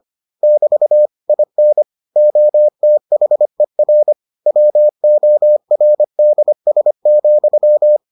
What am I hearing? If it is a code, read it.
Morse code, heard as a keyed tone, about 25 wpm, 600 Hz. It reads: <BT> IN OTHER WORDS,